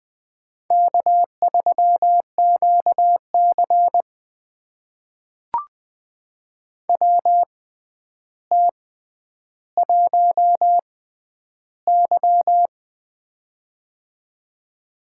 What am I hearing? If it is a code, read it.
Morse code, heard as a keyed tone, about 20 wpm, spaced as for 10 wpm, 700 Hz. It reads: K3QC WT1Y
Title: USA Call Signs - 4 Characters - 6x Character Spacing - 20wpm